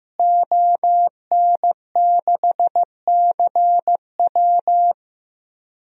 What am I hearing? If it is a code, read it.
Morse code, heard as a keyed tone, about 15 wpm, 700 Hz. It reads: ON6CW
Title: DXCC-100 Call Signs - 5 Characters - Repeated 3x - 15wpm